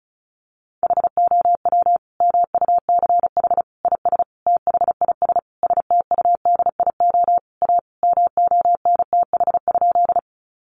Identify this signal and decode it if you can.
Morse code, heard as a keyed tone, about 34 wpm, 700 Hz. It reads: HOW MUCH IS THIS STUDIO A MONTH?
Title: Sentences from Top 1200 Words 35wpm